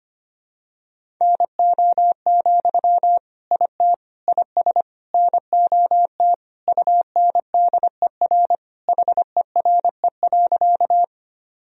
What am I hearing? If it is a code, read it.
Morse code, heard as a keyed tone, about 25 wpm, 700 Hz. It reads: NO, IT IS NOT UNDER HERE.
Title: Sentences from Top 1000 Words - Review Part 1 - 25wpm